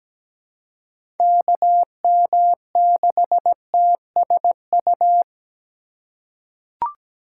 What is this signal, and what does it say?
Morse code, heard as a keyed tone, about 17 wpm, 700 Hz. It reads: KM6TSU
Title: USA Call Signs - 6 Characters - 4x Character Spacing - 17wpm